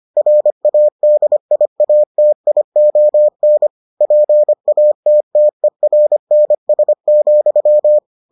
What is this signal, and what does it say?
Morse code, heard as a keyed tone, about 25 wpm, 600 Hz. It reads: RADIATION PATTERNS,